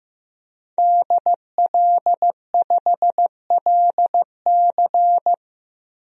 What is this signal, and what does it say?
Morse code, heard as a keyed tone, about 15 wpm, 700 Hz. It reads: DL5LC
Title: DXCC-100 Call Signs - 5 Characters - 15wpm